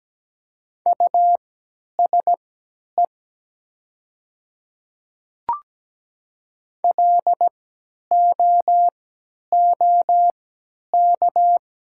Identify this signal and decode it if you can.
Morse code, heard as a keyed tone, about 17 wpm, spaced as for 10 wpm, 700 Hz. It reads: USE LOOK
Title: 3x Character Spacing - Top 200 Words - 17wpm